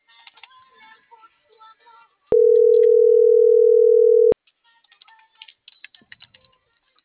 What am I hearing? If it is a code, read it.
Ringback tone (North America), 440 + 480 Hz.